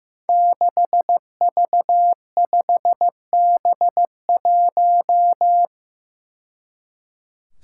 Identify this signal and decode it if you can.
Morse code, heard as a keyed tone, about 15 wpm, 700 Hz. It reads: 6V5B1
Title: Five Challenging Letter-Number Combinations 15wpm